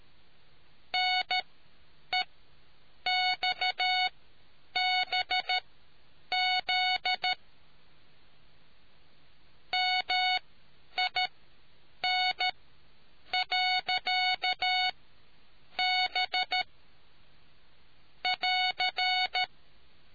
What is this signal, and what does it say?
Morse code, heard as a keyed tone, about 13 wpm, spaced as for 8 wpm, 740 Hz. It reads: NEXBZ MIN.B <AR>